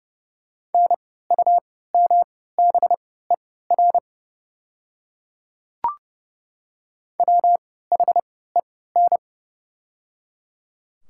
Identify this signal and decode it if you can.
Morse code, heard as a keyed tone, about 30 wpm, spaced as for 17 wpm, 700 Hz. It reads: NUMBER WHEN
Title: 3x Character Spacing - Top 100 Words - 30wpm